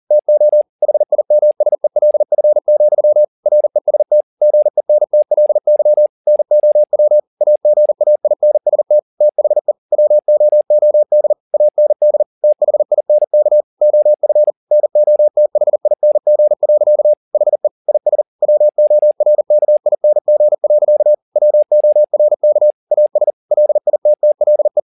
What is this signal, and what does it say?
Morse code, heard as a keyed tone, about 40 wpm, 600 Hz. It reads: TO HIMSELF, REST GENTLY NOW AGAINST THE WOOD AND THINK OF NOTHING. HE IS WORKING. WORK AS LITTLE